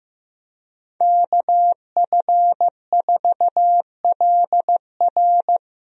Morse code, heard as a keyed tone, about 16 wpm, 700 Hz. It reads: KF4LR
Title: USA Call Signs - 5 Characters - Repeated 3x - 15wpm